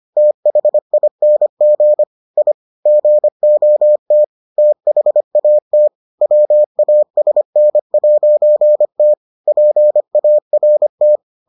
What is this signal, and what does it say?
Morse code, heard as a keyed tone, about 25 wpm, 600 Hz. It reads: THING I GOT THAT WASN'T PART